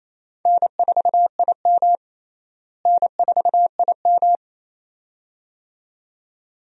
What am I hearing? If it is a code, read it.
Morse code, heard as a keyed tone, about 28 wpm, 700 Hz. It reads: N4IM N4IM
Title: USA Call Signs - 4 Characters - Repeated 2x - 28wpm